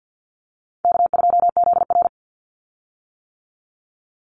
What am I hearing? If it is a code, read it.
Morse code, heard as a keyed tone, about 50 wpm, 700 Hz. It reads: K2ZR
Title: Easy Call Signs - 50wpm